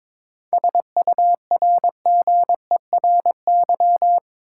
Morse code, heard as a keyed tone, about 22 wpm, 700 Hz. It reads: SURGERY
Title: Seven Letter Words - 22wpm